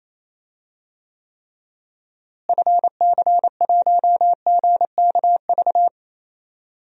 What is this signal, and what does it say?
Morse code, heard as a keyed tone, about 27 wpm, 700 Hz. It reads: FC1GKV
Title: DXCC-100 Call Signs - 6 Characters - Repeated 2x - 28wpm